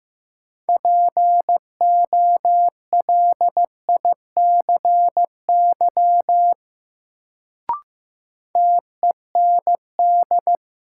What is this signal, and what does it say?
Morse code, heard as a keyed tone, about 16 wpm, 700 Hz. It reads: POLICY TEND
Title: Sentences from Top 1000 Words - Review Part 5 - New Word Drill - 15wpm